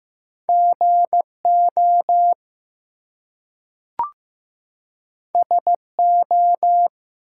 Morse code, heard as a keyed tone, about 15 wpm, 700 Hz. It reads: GO SO
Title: Two Letter Words - 15wpm